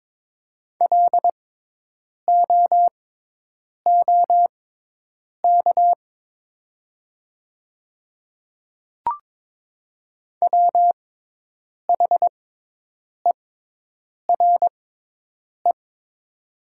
Morse code, heard as a keyed tone, about 22 wpm, spaced as for 8 wpm, 700 Hz. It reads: LOOK WHERE
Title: 6x Character Spacing - Top 300 Words - 22wpm